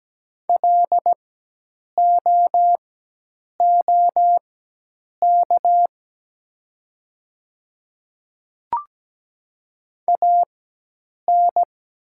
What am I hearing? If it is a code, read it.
Morse code, heard as a keyed tone, about 17 wpm, spaced as for 8 wpm, 700 Hz. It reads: LOOK AN
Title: 4x Character Spacing - Top 100 Words - 17wpm